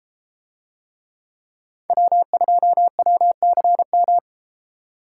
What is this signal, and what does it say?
Morse code, heard as a keyed tone, about 31 wpm, 700 Hz. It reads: W2WCM